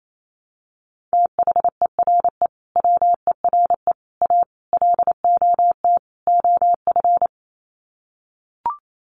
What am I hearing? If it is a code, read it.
Morse code, heard as a keyed tone, about 28 wpm, 700 Hz. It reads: THERE WERE A LOT OF
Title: Sets of 5 Words 28wpm